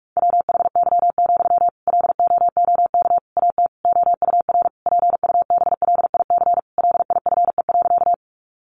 Morse code, heard as a keyed tone, about 44 wpm, 700 Hz. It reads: WHY, LOOK AT OUR PUBLIC LIFE.
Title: Sentences from Top 600 Words 45wpm